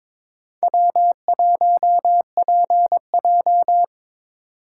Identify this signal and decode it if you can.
Morse code, heard as a keyed tone, about 22 wpm, 700 Hz. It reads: W1PJ
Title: USA Call Signs - 3x Character Spacing - 22wpm